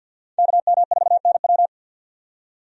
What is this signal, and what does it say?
Morse code, heard as a keyed tone, about 53 wpm, 700 Hz. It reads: KM4NW
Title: USA Call Signs - 5 Characters - 50wpm